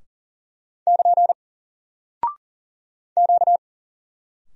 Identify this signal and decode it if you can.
Morse code, heard as a keyed tone, about 39 wpm, 700 Hz. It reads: <KN> Q
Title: Single Character (Letter:Number:Punctuation:Prosign) 40wpm